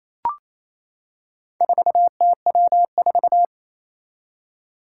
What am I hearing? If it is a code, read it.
Morse code, heard as a keyed tone, about 28 wpm, 700 Hz. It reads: 4TW4
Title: Four Challenging Letter-Number Combinations 28wpm